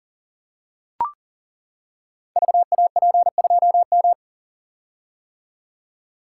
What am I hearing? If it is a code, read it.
Morse code, heard as a keyed tone, about 40 wpm, 700 Hz. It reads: VAP2M
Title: Code Groups - 5 Letter:Numbers - 40wpm